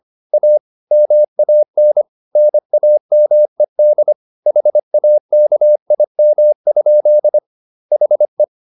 Morse code, heard as a keyed tone, about 25 wpm, 600 Hz. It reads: A MAN NAMED HAKIM? HE